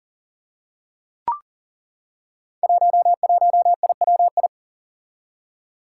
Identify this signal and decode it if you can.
Morse code, heard as a keyed tone, about 38 wpm, 700 Hz. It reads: 11IWI
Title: Five Challenging Letter-Number Combinations 40wpm